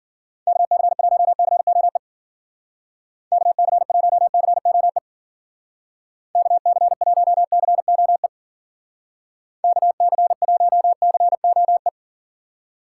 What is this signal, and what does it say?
Morse code, heard as a keyed tone, about 48 wpm, 700 Hz. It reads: KC1COE KC1COE KC1COE KC1COE
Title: Speed Racing - USA Call Signs 40wpm